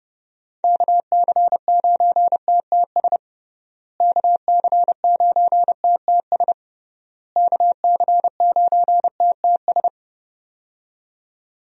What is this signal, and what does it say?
Morse code, heard as a keyed tone, about 32 wpm, 700 Hz. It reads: KC9TTS KC9TTS KC9TTS
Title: USA Call Signs - 6 Characters - Repeated 3x - 30wpm